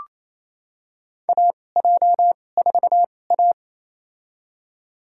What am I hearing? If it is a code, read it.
Morse code, heard as a keyed tone, about 28 wpm, spaced as for 20 wpm, 700 Hz. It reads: AJ4A
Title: USA Call Signs - 4 Characters - 2x Character Spacing - 28wpm